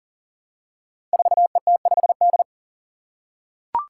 Morse code, heard as a keyed tone, about 38 wpm, 700 Hz. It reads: 4ET5D
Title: Five Challenging Letter-Number Combinations 40wpm